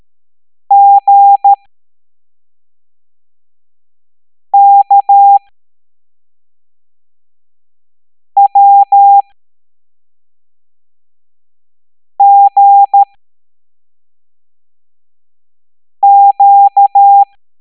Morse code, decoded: GKWGQ